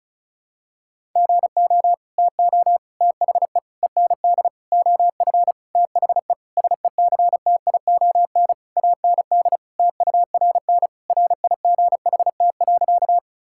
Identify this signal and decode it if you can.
Morse code, heard as a keyed tone, about 36 wpm, 700 Hz. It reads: GO TO THE END OF THE SECTION AND TURN RIGHT.
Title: Sentences from Top 1000 Words - Review Part 5 - 35wpm